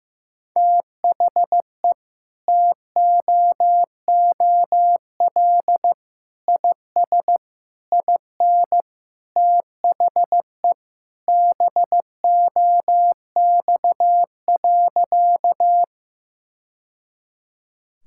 Morse code, heard as a keyed tone, about 15 wpm, 700 Hz. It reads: THE TOOL IS IN THE BOX.
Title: Sentences from Top 1000 Words - Review Part 6 - 15wpm